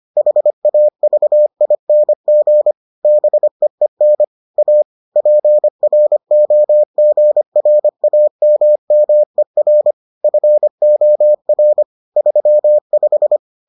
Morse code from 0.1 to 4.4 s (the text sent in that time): HAVING BEEN